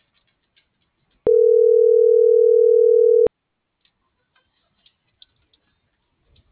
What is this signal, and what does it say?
Ringback tone (North America), 440 + 480 Hz.